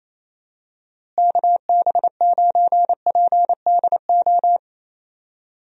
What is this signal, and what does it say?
Morse code, heard as a keyed tone, about 27 wpm, 700 Hz. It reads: KB9PDO
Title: USA Call Signs - 6 Characters - Repeated 3x - 28wpm